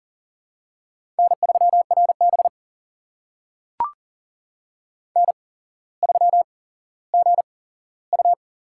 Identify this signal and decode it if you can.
Morse code, heard as a keyed tone, about 40 wpm, spaced as for 17 wpm, 700 Hz. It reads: N3RB N3GU
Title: USA Call Signs - 4 Characters - 8x Character Spacing - 40wpm